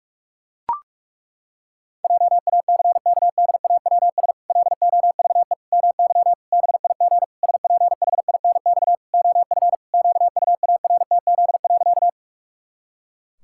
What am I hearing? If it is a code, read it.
Morse code, heard as a keyed tone, about 41 wpm, 700 Hz. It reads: JACKDAWS LOVE MY BIG SPHINX OF QUARTZ.
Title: Pangram Challenge v2 - 45wpm